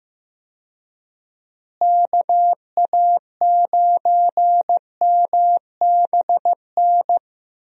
Morse code, heard as a keyed tone, about 15 wpm, 700 Hz. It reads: KA9MBN
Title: USA Call Signs - 6 Characters - Repeated 2x - 15wpm